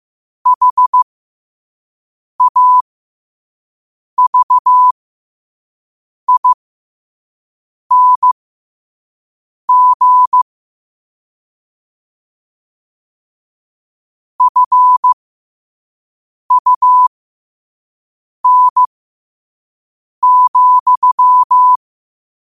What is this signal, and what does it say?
Morse code, heard as a keyed tone, about 15 wpm, spaced as for 5 wpm, 1000 Hz. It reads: HAVING FUN,
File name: qotdfiles / 5wpm